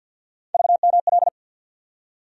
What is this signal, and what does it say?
Morse code, heard as a keyed tone, about 51 wpm, 700 Hz. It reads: VML
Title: USA Call Sign Postfixes - 3 Letters 50wpm